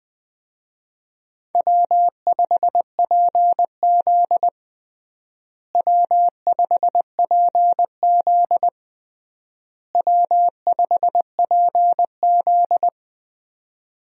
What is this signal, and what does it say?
Morse code, heard as a keyed tone, about 20 wpm, 700 Hz. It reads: W5PZ W5PZ W5PZ
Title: USA Call Signs - 4 Characters - Repeated 3x - 20wpm